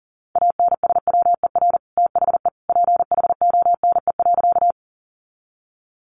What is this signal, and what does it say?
Morse code, heard as a keyed tone, about 40 wpm, 700 Hz. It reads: ANSWER THE PHONE.
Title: Sentences from Top 1100 Words 40wpm